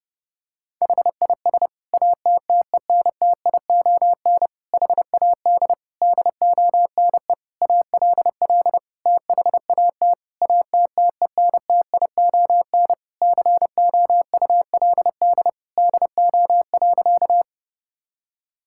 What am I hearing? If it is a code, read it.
Morse code, heard as a keyed tone, about 30 wpm, 700 Hz. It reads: HIS ATTENTION HAD DONE ALL THAT ATTENTION COULD DO.
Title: Sentences from Top 1000 Words - Review Part 6 - 30wpm